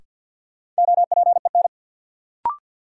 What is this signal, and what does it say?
Morse code, heard as a keyed tone, about 50 wpm, 700 Hz. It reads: OPEN
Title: Sentences from Top 300 Words - New Word Drill - 50wpm